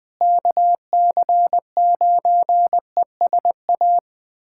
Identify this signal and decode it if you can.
Morse code, heard as a keyed tone, about 21 wpm, 700 Hz. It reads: KC9ESA
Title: USA Callsign DE Callsign - 6 Characters - 20wpm